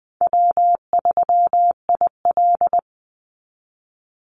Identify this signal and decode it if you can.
Morse code, heard as a keyed tone, about 20 wpm, 700 Hz. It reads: W3IL
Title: Easy Call Signs Repeated 3x - 20wpm